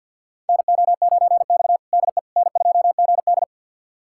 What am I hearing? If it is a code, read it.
Morse code, heard as a keyed tone, about 51 wpm, 700 Hz. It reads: NO9X DE N2GD
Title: USA Callsign DE Callsign - 4 Characters - 50wpm